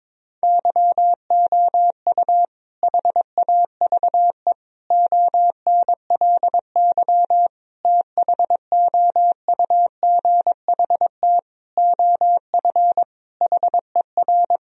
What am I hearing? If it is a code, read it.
Morse code, heard as a keyed tone, about 22 wpm, 700 Hz. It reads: YOU HAVE ONLY THOUGHT OF HER
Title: Sentences from Top 1000 Words - Review Part 2 - 22wpm